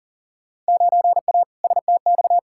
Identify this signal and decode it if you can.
Morse code, heard as a keyed tone, about 40 wpm, 700 Hz. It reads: 9A STX
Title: Field Day Exchanges - 40wpm